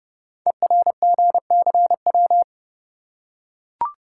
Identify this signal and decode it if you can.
Morse code, heard as a keyed tone, about 30 wpm, 700 Hz. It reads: ERGCW
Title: Code Groups - 5 Letter:Numbers - 30wpm